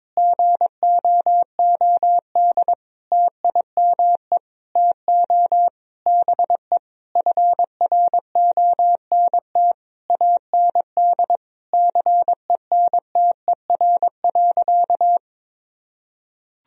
Morse code, heard as a keyed tone, about 22 wpm, 700 Hz. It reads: GOOD TIME TO BE FRONT AND CENTER.
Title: Sentences from Top 1000 Words - Review Part 2 - 22wpm